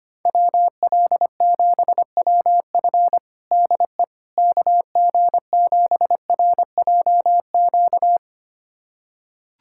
Morse code, heard as a keyed tone, about 25 wpm, 700 Hz. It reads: WL7WF DE KG7RJQ
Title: USA Callsign DE Callsign 25wpm